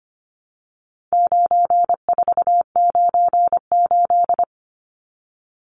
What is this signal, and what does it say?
Morse code, heard as a keyed tone, about 25 wpm, 700 Hz. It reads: 9498
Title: Numbers - 4-Digit 25wpm